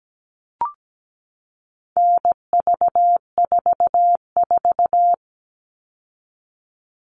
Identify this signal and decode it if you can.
Morse code, heard as a keyed tone, about 17 wpm, 700 Hz. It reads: NV44
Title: All Cut Numbers - 4-Digit 17wpm